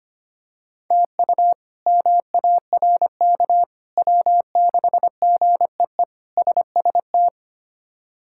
Morse code, heard as a keyed tone, about 25 wpm, 700 Hz. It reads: TU MARK W6GEE SST
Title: K1USN SST QSO Practice - 25wpm